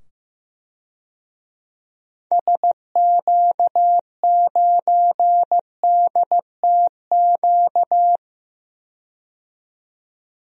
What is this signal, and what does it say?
Morse code, heard as a keyed tone, about 15 wpm, 700 Hz. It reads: SQ9DTQ
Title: DXCC-100 Call Signs - 6 Characters - 15wpm